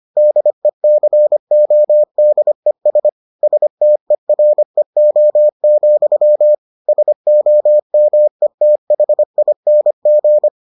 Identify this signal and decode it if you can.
Morse code, heard as a keyed tone, about 25 wpm, 600 Hz. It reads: DECODES STEREO, SOMETHING